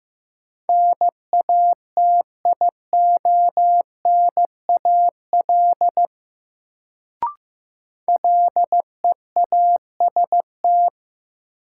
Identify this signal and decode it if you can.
Morse code, heard as a keyed tone, about 15 wpm, 700 Hz. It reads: NATIONAL LEAST
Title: Sets of 2 Words - New Word Drill - 15wpm